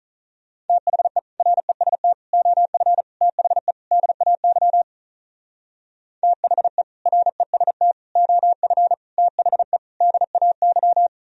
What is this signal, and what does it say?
Morse code, heard as a keyed tone, about 43 wpm, 700 Hz. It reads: THE REST OF THE DAY THE REST OF THE DAY
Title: Speed-Racing - Sets of 5 Words 35wpm